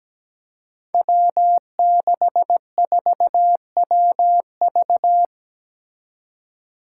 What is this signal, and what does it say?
Morse code, heard as a keyed tone, about 17 wpm, 700 Hz. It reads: W64WV
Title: Five Challenging Letter-Number Combinations 17wpm